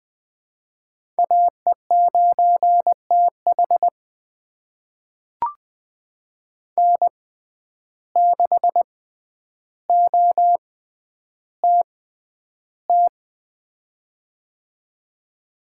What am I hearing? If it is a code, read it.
Morse code, heard as a keyed tone, about 20 wpm, spaced as for 10 wpm, 700 Hz. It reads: AE9TH N6OTT
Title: USA Call Signs - 5 Characters - 6x Character Spacing - 20wpm